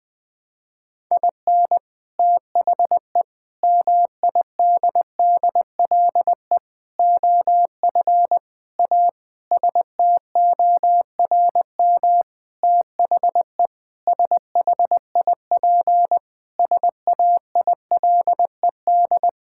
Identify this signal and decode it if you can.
Morse code, heard as a keyed tone, about 20 wpm, 700 Hz. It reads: IN THE MIDDLE OF A STORM THE SHIP SAILED
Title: Sentences with 5-Word Phrases - Spoken Beginning - 20wpm